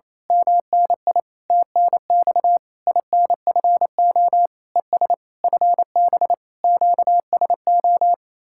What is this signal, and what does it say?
Morse code, heard as a keyed tone, about 26 wpm, 700 Hz. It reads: MNI TNX INFO ES FB QSO
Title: Speed-Racing Common QSO Phrases - 28wpm